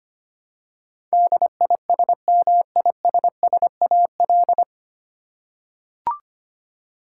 Morse code, heard as a keyed tone, about 25 wpm, 700 Hz. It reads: DISMISSAL
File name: practice / 25wpm / 9 Letter Words - First 3 Letters Given - 25wpm